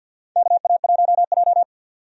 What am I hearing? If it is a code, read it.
Morse code, heard as a keyed tone, about 47 wpm, 700 Hz. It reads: KA1J
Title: USA Call Signs - 4 Characters - 50wpm